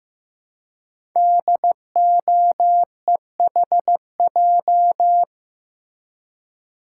Morse code, heard as a keyed tone, about 15 wpm, 700 Hz. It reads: DOEHJ
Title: Code Groups - 5 Letters - 15wpm